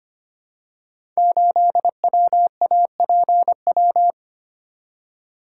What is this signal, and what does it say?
Morse code, heard as a keyed tone, about 25 wpm, 700 Hz. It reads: 8WAPW
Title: Code Groups - 5 Letter:Numbers - 25wpm